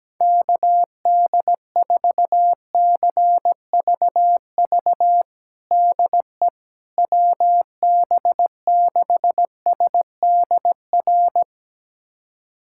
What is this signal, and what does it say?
Morse code, heard as a keyed tone, about 17 wpm, 700 Hz. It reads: KD4CVV DE WB6SDR
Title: USA Callsign DE Callsign - 6 Characters - 17wpm